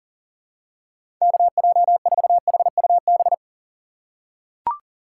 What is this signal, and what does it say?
Morse code, heard as a keyed tone, about 40 wpm, 700 Hz. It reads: KJ4HUB
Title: USA Call Signs - 6 Characters - 2x Character Spacing - 40wpm